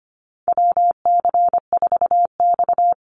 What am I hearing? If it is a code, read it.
Morse code, heard as a keyed tone, about 25 wpm, 700 Hz. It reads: WC4X
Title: Call Signs Repeated 3x - 25wpm